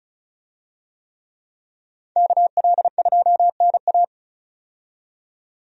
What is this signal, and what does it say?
Morse code, heard as a keyed tone, about 34 wpm, 700 Hz. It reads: KL2NA
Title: USA Call Signs - 5 Characters - 35wpm